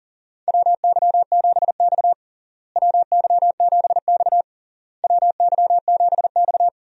Morse code, heard as a keyed tone, about 40 wpm, 700 Hz. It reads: WY7X WY7X WY7X
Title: USA Call Signs - 4 Characters - Repeated 3x - 40wpm